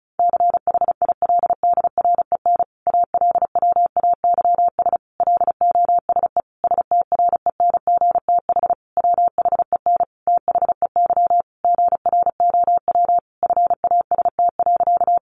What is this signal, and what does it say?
Morse code, heard as a keyed tone, about 34 wpm, 700 Hz. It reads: CHILDREN ALWAYS LOSE STRENGTH WHEN THEY GROW FAST.
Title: Sentences from Top 1300 Words 35wpm